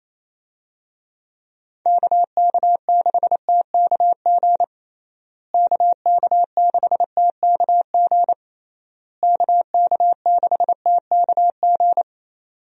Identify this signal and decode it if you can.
Morse code, heard as a keyed tone, about 28 wpm, 700 Hz. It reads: KK6TKG KK6TKG KK6TKG
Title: USA Call Signs - 6 Characters - Repeated 3x - 28wpm